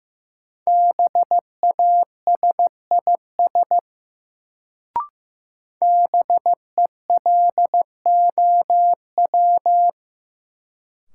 Morse code, heard as a keyed tone, about 15 wpm, 700 Hz. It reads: BASIS BELOW
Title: Sentences from Top 1000 Words - Review Part 5 - New Word Drill - 15wpm